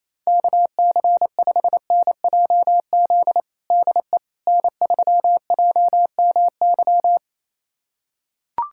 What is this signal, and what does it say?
Morse code, heard as a keyed tone, about 28 wpm, 700 Hz. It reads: KC5NJZ DE N3JMY
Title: USA Callsign DE Callsign 28wpm